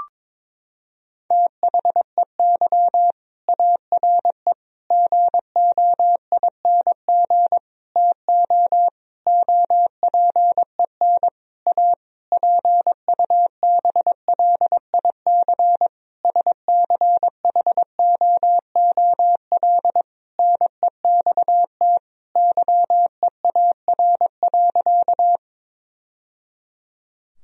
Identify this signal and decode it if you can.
Morse code, heard as a keyed tone, about 22 wpm, 700 Hz. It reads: THEY ARE GOING TO OPEN A PUBLIC SCHOOL NEXT YEAR.
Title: Sentences from Top 1000 Words - Review Part 3 - 22wpm